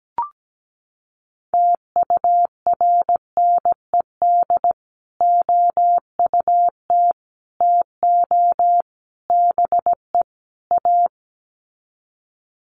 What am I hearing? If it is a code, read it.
Morse code, heard as a keyed tone, about 17 wpm, 700 Hz. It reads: TURNED OUT TO BE A
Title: Sets of 5 Words 17wpm